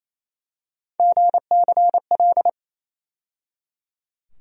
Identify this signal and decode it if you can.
Morse code, heard as a keyed tone, about 28 wpm, 700 Hz. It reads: GCL
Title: USA Call Sign Postfixes - 3 Letters 28wpm